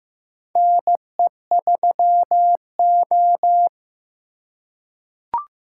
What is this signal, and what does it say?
Morse code, heard as a keyed tone, about 15 wpm, 700 Hz. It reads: NE3O
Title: USA Call Signs - 4 Characters - 3x Character Spacing - 15wpm